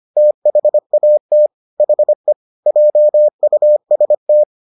Morse code, heard as a keyed tone, about 25 wpm, 600 Hz. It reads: THAT HE JUST